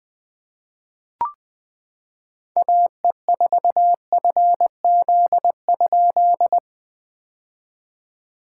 Morse code, heard as a keyed tone, about 20 wpm, 700 Hz. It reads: AE4FZ?
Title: POTA QSO Practice - 20wpm